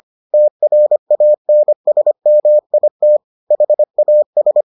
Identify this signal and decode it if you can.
Morse code, heard as a keyed tone, about 25 wpm, 600 Hz. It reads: TRANSMIT HAS